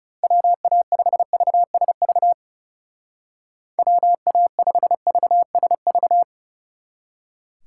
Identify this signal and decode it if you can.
Morse code, heard as a keyed tone, about 36 wpm, 700 Hz. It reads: WA5VSV WA5VSV
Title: Speed Racing - USA Call Signs 30wpm